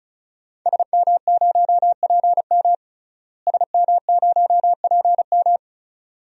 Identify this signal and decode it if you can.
Morse code, heard as a keyed tone, about 36 wpm, 700 Hz. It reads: SM0PM SM0PM
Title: DXCC-100 Call Signs - 5 Characters - Repeated 2x - 35wpm